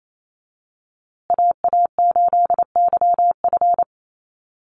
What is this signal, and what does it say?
Morse code, heard as a keyed tone, about 28 wpm, 700 Hz. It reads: AA8YF
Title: USA Call Signs Repeated 3x 28wpm